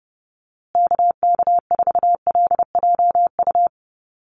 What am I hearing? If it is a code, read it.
Morse code, heard as a keyed tone, about 30 wpm, 700 Hz. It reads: KK4LJU
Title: USA Call Signs Repeated 2x 30wpm